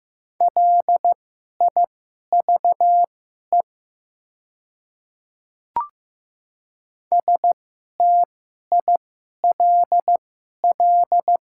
Morse code, heard as a keyed tone, about 15 wpm, spaced as for 11 wpm, 700 Hz. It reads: LIVE STILL
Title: 2x Character Spacing - Top 500 Words - 15wpm